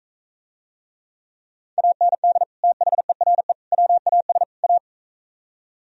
Morse code, heard as a keyed tone, about 42 wpm, 700 Hz. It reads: AND THERE WAS A